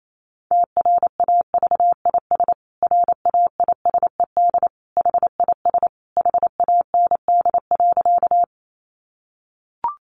Morse code, heard as a keyed tone, about 28 wpm, 700 Hz. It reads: TRAVIS RAISED HIS HAND.
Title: A Sound Of Thunder 28wpm